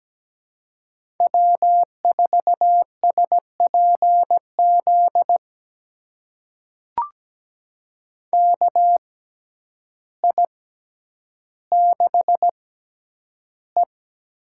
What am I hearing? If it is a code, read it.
Morse code, heard as a keyed tone, about 17 wpm, spaced as for 9 wpm, 700 Hz. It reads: W4SPZ KI6E
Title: USA Call Signs - 5 Characters - 6x Character Spacing - 17wpm